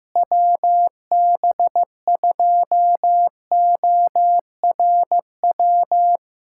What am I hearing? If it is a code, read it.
Morse code, heard as a keyed tone, about 15 wpm, 700 Hz. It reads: WB2ORW
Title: USA Callsign DE Callsign 15wpm